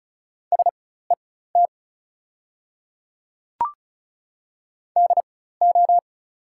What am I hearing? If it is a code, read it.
Morse code, heard as a keyed tone, about 34 wpm, spaced as for 16 wpm, 690 Hz. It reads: SET DO
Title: 4x Character Spacing - Top 200 Words - 35wpm